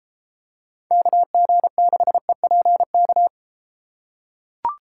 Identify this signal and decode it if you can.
Morse code, heard as a keyed tone, about 32 wpm, 700 Hz. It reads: KG6EPK